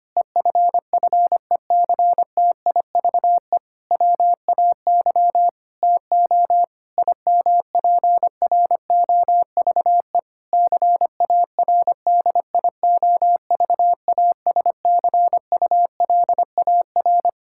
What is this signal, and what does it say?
Morse code, heard as a keyed tone, about 25 wpm, 700 Hz. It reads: EFFECTIVE WAY TO IMPROVE CARDIOVASCULAR
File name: practice / 25wpm / Difficult Sentences with Hints - 25wpm